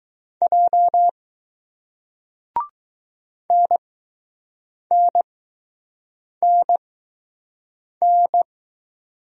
Morse code, heard as a keyed tone, about 20 wpm, spaced as for 6 wpm, 700 Hz. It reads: J NNNN